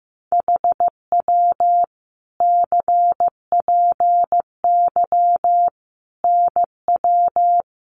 Morse code, heard as a keyed tone, about 15 wpm, 700 Hz. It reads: HW CPY NW
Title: Common QSO Phrases 15wpm